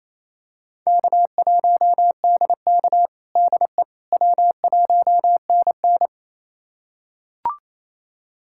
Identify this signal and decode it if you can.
Morse code, heard as a keyed tone, about 28 wpm, 700 Hz. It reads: K1DK DE W1NN
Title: USA Callsign DE Callsign - 4 Characters - 28wpm